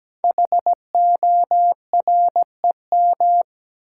Morse code, heard as a keyed tone, about 17 wpm, 700 Hz. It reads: HOREM
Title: Code Groups - 5 Letters - 17wpm